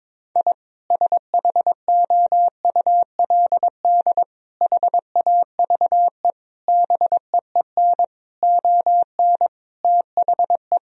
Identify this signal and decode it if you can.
Morse code, heard as a keyed tone, about 22 wpm, 700 Hz. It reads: I SHOULD HAVE BEEN ON THE
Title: Sentences from Top 1000 Words - Review Part 1 - 22wpm